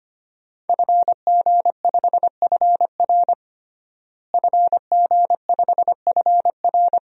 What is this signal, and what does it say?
Morse code, heard as a keyed tone, about 25 wpm, 700 Hz. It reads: FG5FR FG5FR
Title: DXCC-100 Call Signs - 5 Characters - Repeated 2x - 25wpm